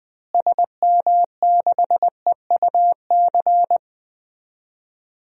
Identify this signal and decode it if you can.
Morse code, heard as a keyed tone, about 20 wpm, 700 Hz. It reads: SM6EUC
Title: DXCC-100 Call Signs - 6 Characters - Repeated 2x - 20wpm